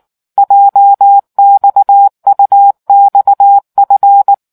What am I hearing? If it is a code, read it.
Morse code, heard as a keyed tone, about 19 wpm, 800 Hz. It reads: JXUXF